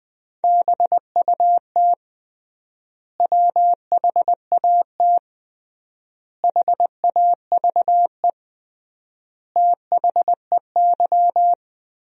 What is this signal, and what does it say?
Morse code, heard as a keyed tone, about 20 wpm, 700 Hz. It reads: BUT WHAT HAVE THEY
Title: 3x Word-Spacing - Sentences from Top 100 Words - 20wpm